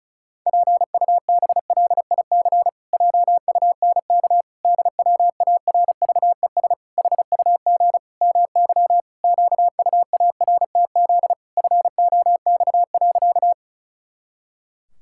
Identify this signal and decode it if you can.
Morse code, heard as a keyed tone, about 36 wpm, 700 Hz. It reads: PUBLIC JUNK DWARVES HUG MY QUARTZ FOX.
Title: Pangram Challenge v2 - 35wpm